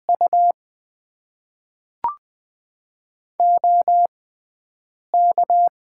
Morse code, heard as a keyed tone, about 20 wpm, spaced as for 7 wpm, 700 Hz. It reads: U OK
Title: Top 90 QSO Elements - 6x Character Spacing - 20wpm